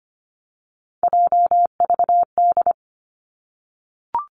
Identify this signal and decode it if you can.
Morse code, heard as a keyed tone, about 25 wpm, 700 Hz. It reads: JVD
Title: Three Challenging Letter-Number Combinations 25wpm